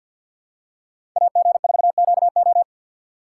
Morse code, heard as a keyed tone, about 53 wpm, 700 Hz. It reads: AG4QO
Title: USA Call Signs - 5 Characters - 8x Character Spacing - 50wpm